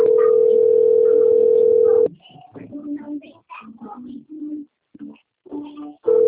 Ringback tone (North America), 440 + 480 Hz.
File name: out- - - - - .9502